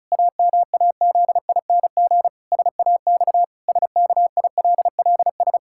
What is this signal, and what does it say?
Morse code, heard as a keyed tone, about 36 wpm, 700 Hz. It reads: AMAZING SAX SKILLS
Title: Pangram Challenge v2 - 35wpm